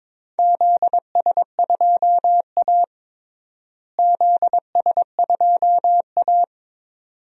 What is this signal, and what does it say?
Morse code, heard as a keyed tone, about 22 wpm, 700 Hz. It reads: ZS2A ZS2A
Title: DXCC-100 Call Signs - 4 Characters - Repeated 2x - 22wpm